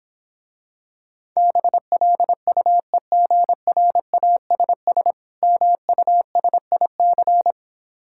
Morse code, heard as a keyed tone, about 26 wpm, 700 Hz. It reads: BLUEGRASS MUSIC